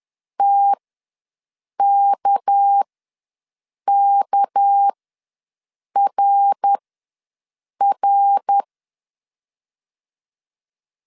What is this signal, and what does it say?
Morse code, heard as a keyed tone, about 10 wpm, spaced as for 6 wpm, 800 Hz. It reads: TKKRR
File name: LEZIONI / 6 PKX0 / 6 PK ws 10wpm 3745